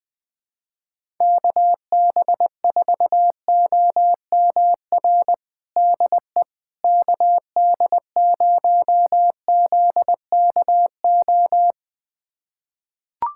Morse code, heard as a keyed tone, about 20 wpm, 700 Hz. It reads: KB4OMR DE KD0ZKO
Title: USA Callsign DE Callsign - 6 Characters - 20wpm